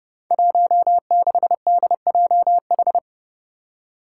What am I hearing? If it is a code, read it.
Morse code, heard as a keyed tone, about 30 wpm, 700 Hz. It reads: 16DJH
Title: Five Challenging Letter-Number Combinations 30wpm